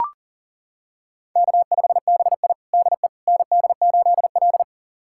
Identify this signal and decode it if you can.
Morse code, heard as a keyed tone, about 40 wpm, 700 Hz. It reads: K5BI DE ND8L
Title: USA Callsign DE Callsign - 4 Characters - 40wpm